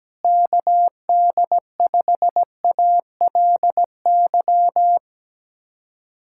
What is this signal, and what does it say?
Morse code, heard as a keyed tone, about 17 wpm, 700 Hz. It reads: KD5ALY
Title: USA Callsign DE Callsign 17wpm